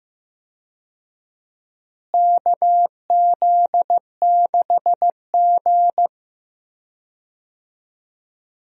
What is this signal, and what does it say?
Morse code, heard as a keyed tone, about 15 wpm, 700 Hz. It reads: KZ6G
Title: USA Call Signs - 4 Characters - Repeated 2x - 15wpm